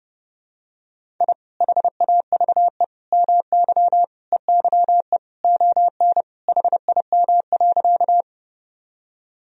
Morse code, heard as a keyed tone, about 30 wpm, 700 Hz. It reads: I HAVE MY EYE ON HIM.
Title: Sentences from Top 1000 Words - Review Part 1 - 30wpm